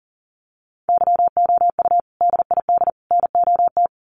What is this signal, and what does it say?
Morse code, heard as a keyed tone, about 40 wpm, 700 Hz. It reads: YOU DID NOT
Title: Sets of 3 Words 40wpm